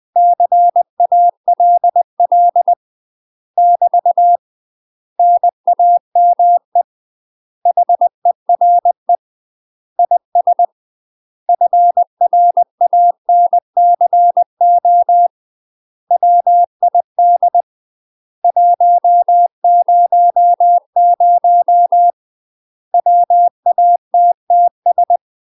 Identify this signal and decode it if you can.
Morse code, heard as a keyed tone, about 20 wpm, 700 Hz. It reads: CALL <BT> NAME HERE IS FRANCO WID 100 WATTS
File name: UK MASTER QSOs TRAINING / 20WPM / UK MASTER QSOs 0034